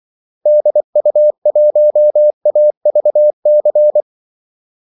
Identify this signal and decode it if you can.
Morse code, heard as a keyed tone, about 24 wpm, 600 Hz. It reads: DU1AVC
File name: Callsigns-24wpm